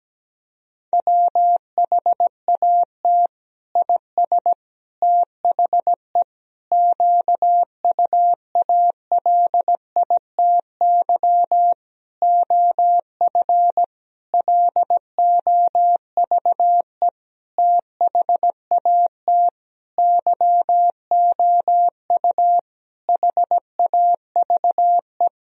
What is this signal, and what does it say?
Morse code, heard as a keyed tone, about 17 wpm, 700 Hz. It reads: WHAT IS THE QUALITY OF LOVE THAT YOU HAVE
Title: Sentences from Top 1000 Words - Review Part 4 - 17wpm